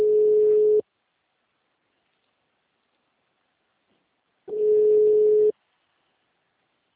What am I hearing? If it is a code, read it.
Ringback tone (Europe, 425 Hz).
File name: out- -2013- - - .17541